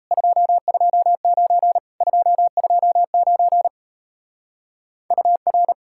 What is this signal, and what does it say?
Morse code, heard as a keyed tone, about 37 wpm, 700 Hz. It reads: 229 229 UR